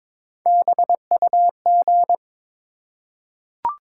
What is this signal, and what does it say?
Morse code, heard as a keyed tone, about 22 wpm, 700 Hz. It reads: BUG
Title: Three Letter Words - 22wpm